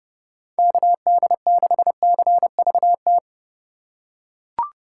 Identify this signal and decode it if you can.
Morse code, heard as a keyed tone, about 30 wpm, 700 Hz. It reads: KD6CVT
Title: USA Call Signs - 6 Characters - 4x Character Spacing - 30wpm